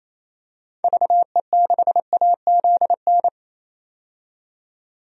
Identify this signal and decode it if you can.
Morse code, heard as a keyed tone, about 28 wpm, 700 Hz. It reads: VE6AZN
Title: DXCC-100 Call Signs - 6 Characters - 28wpm